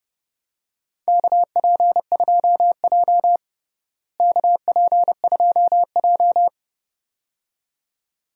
Morse code, heard as a keyed tone, about 30 wpm, 700 Hz. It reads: KP2J KP2J
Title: DXCC-100 Call Signs - 4 Characters - Repeated 2x - 30wpm